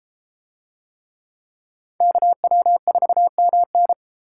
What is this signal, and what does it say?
Morse code, heard as a keyed tone, about 32 wpm, 700 Hz. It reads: KW4MN